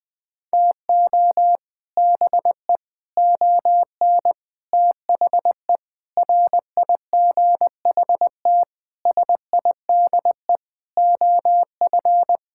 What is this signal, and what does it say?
Morse code, heard as a keyed tone, about 20 wpm, 700 Hz. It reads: TO BE ON THE RIGHT SIDE OF
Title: Sets of 8 Words - 20wpm